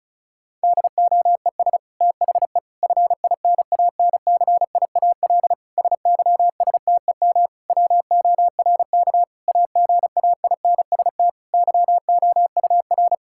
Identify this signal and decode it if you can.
Morse code, heard as a keyed tone, about 34 wpm, 700 Hz. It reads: DOES THE FINANCIAL SYSTEM WORK AGAINST YOUR
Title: Sentences from Top 1000 Words - Review Part 5 - 35wpm